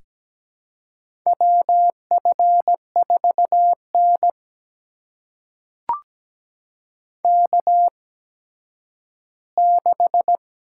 Morse code, heard as a keyed tone, about 17 wpm, spaced as for 10 wpm, 700 Hz. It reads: WF4N K6
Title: USA Call Signs - 4 Characters - 8x Character Spacing - 17wpm